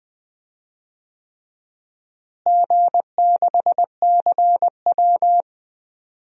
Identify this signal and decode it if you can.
Morse code, heard as a keyed tone, about 20 wpm, 700 Hz. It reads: G6CW
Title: DXCC-100 Call Signs - 4 Characters - Repeated 3x - 20wpm